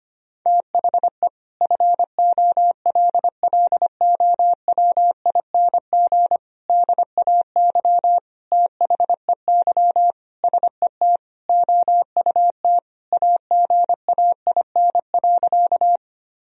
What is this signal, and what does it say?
Morse code, heard as a keyed tone, about 25 wpm, 700 Hz. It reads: THE FOLLOWING DAY THEY SET OUT AGAIN.
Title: Sentences from Top 1000 Words - Review Part 4 - 25wpm